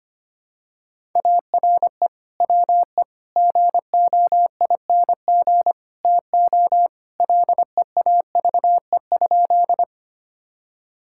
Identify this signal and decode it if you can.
Morse code, heard as a keyed tone, about 25 wpm, 700 Hz. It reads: ARE WE GOING TO LEAVE?
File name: practice / 25wpm / Sentences with 4-Word Phrases - 25wpm